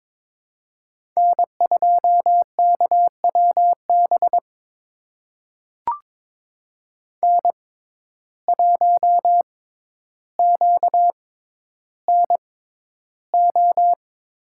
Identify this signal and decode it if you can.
Morse code, heard as a keyed tone, about 22 wpm, spaced as for 11 wpm, 700 Hz. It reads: N2KWB N1QNO
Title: USA Call Signs - 6x Character Spacing - 22wpm